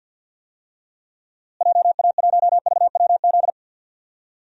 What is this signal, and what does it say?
Morse code, heard as a keyed tone, about 51 wpm, 700 Hz. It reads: JA1VWZ
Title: DXCC-100 Call Signs - 6 Characters - Repeated 3x - 50wpm